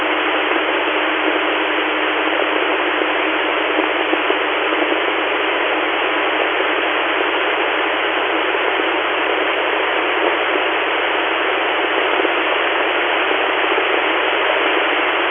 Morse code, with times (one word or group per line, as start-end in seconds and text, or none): none